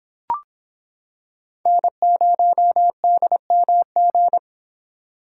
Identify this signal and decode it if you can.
Morse code, heard as a keyed tone, about 26 wpm, 700 Hz. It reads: N0DMG